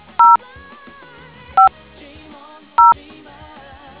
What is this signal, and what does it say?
DTMF, heard as telephone touch tones, tones of about 170 ms. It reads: 0 5 0